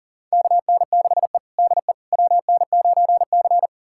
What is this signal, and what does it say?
Morse code, heard as a keyed tone, about 40 wpm, 700 Hz. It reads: KN6E DE WN9C
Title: USA Callsign DE Callsign - 4 Characters - 40wpm